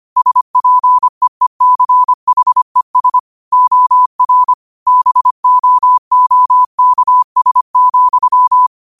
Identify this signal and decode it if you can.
Morse code, heard as a keyed tone, about 25 wpm, 1000 Hz. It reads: SPEECHES OR BOOKS,
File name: qotdfiles / 25wpm